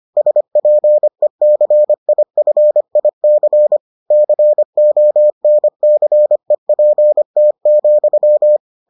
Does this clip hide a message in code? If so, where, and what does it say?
Yes: Morse code from 0.2 to 8.6 s, heard as a keyed tone, about 25 wpm, 600 Hz. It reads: SPECIFIC CONCEPT,